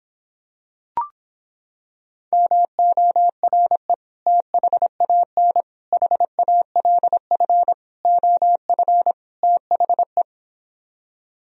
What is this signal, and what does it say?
Morse code, heard as a keyed tone, about 26 wpm, 700 Hz. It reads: MORE THAN HALF OF THE